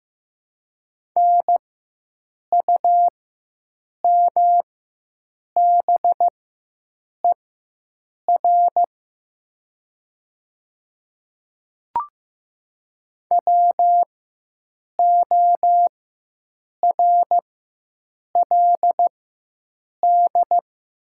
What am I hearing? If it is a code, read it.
Morse code, heard as a keyed tone, about 15 wpm, spaced as for 7 wpm, 700 Hz. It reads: NUMBER WORLD
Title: 4x Character Spacing - Top 200 Words - 15wpm